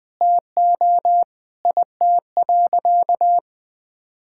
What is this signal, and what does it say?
Morse code, heard as a keyed tone, about 21 wpm, 700 Hz. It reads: TO IT.
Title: Sentences from Top 1000 Words - Review Part 6 - 20wpm